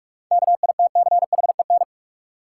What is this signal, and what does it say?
Morse code, heard as a keyed tone, about 43 wpm, 700 Hz. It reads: KITCHEN
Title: Sentences from Top 1400 Words - New Word Drill Alt - 45wpm